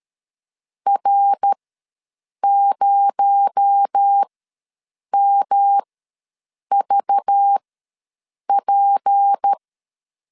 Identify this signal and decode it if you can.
Morse code, heard as a keyed tone, about 13 wpm, spaced as for 7 wpm, 800 Hz. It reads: R0MVP
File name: LEZIONI / 6 PKX0 / 6 PKX0 ws 12wpm 3448